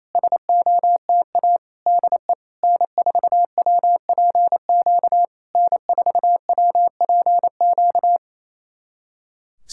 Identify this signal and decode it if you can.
Morse code, heard as a keyed tone, about 28 wpm, 690 Hz. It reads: SOTA DE N4WPQ N4WPQ
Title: SOTA QSO Practice 28wpm